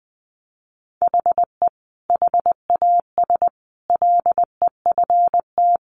Morse code, heard as a keyed tone, about 20 wpm, 700 Hz. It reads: HE HAS LEFT
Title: Sentences from Top 1100 Words 20wpm